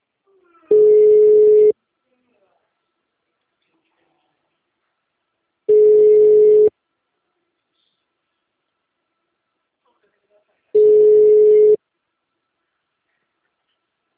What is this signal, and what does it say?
Ringback tone (Europe, 425 Hz).
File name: out- -2008- - - .504